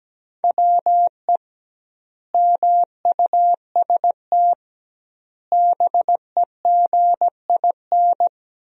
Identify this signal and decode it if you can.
Morse code, heard as a keyed tone, about 17 wpm, 700 Hz. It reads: WE MUST BEGIN
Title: 2x Word-Spacing - Sentences from Top 300 Words - 17wpm